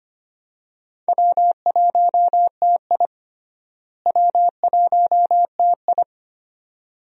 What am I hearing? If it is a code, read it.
Morse code, heard as a keyed tone, about 25 wpm, 700 Hz. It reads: W1TI W1TI
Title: USA Call Signs - 4 Characters - Repeated 2x - 25wpm